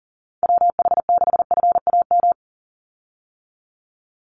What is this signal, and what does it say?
Morse code, heard as a keyed tone, about 40 wpm, 700 Hz. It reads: WH6FAM
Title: USA Call Signs 40wpm